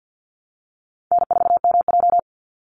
Morse code, heard as a keyed tone, about 50 wpm, 700 Hz. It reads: N4MJ
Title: NA SOTA Chaser Callsigns 50wpm